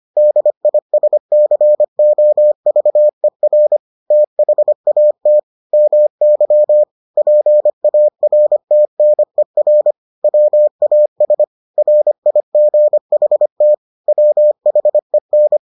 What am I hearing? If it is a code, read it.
Morse code, heard as a keyed tone, about 25 wpm, 600 Hz. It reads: DISCOVER THAT MY PARTNER WAS RIGHT WHEN